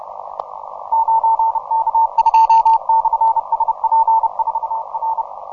Morse code, heard as a keyed tone, about 33 wpm, 910 Hz. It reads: 9K2/SP4R
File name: simple-file-list / v63zm 40m